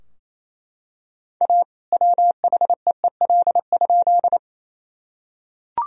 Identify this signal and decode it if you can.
Morse code, heard as a keyed tone, about 28 wpm, 700 Hz. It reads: A WHEEL?
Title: Sentences from Top 1000 Words - Review Part 3 - 28wpm